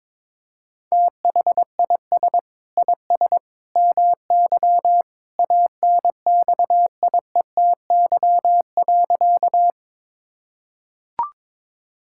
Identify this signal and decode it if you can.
Morse code, heard as a keyed tone, about 22 wpm, 700 Hz. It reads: THIS IS MY ANXIETY.
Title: Sentences from Top 1600 Words 22wpm